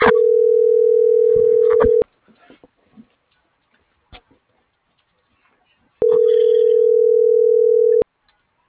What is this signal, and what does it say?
Ringback tone (North America), 440 + 480 Hz.